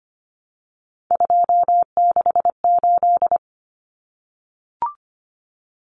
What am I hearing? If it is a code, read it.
Morse code, heard as a keyed tone, about 25 wpm, 700 Hz. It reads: 268
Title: Common Cut Numbers - 3-Digit 25wpm